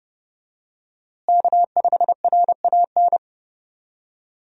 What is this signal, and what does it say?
Morse code, heard as a keyed tone, about 30 wpm, 700 Hz. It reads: K5RAN
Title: USA Call Signs - 5 Characters - Repeated 3x - 30wpm